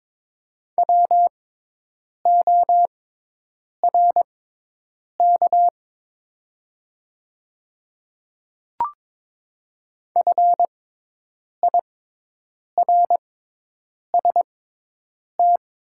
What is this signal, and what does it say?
Morse code, heard as a keyed tone, about 23 wpm, spaced as for 8 wpm, 700 Hz. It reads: WORK FIRST
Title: 6x Character Spacing - Top 300 Words - 22wpm